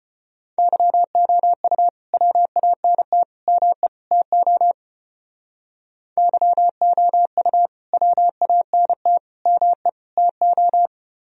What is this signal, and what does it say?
Morse code, heard as a keyed tone, about 32 wpm, 700 Hz. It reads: YOU WANT ME TO YOU WANT ME TO